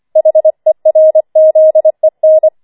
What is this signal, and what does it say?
Morse code, heard as a keyed tone, about 25 wpm, 630 Hz. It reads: HERZEN
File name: 25wpm